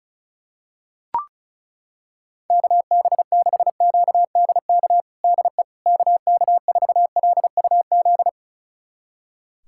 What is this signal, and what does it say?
Morse code, heard as a keyed tone, about 34 wpm, 700 Hz. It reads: KB6QDK DE KK4LUZ
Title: USA Callsign DE Callsign 35wpm